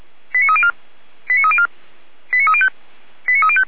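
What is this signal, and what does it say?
Five-tone selective call (ZVEI), heard as quick runs of five tones, tones of about 70 ms. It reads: 89304 89304 89305 89305
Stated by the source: ZVEI five-tone selective call received on an SDR, recorded by the feuerwehr-orlamuende fire brigade.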